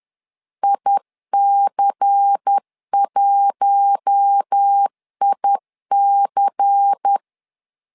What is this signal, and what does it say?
Morse code, heard as a keyed tone, about 10 wpm, 800 Hz. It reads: IC1IC
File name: LEZIONI / 5 LBVC / 5 LBVC 10wpm 2349